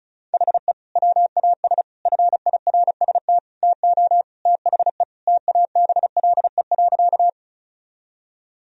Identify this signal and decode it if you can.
Morse code, heard as a keyed tone, about 36 wpm, 700 Hz. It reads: HE WAS FIRST TO THE TABLE.
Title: Sentences from Top 1000 Words - Review Part 2 - 35wpm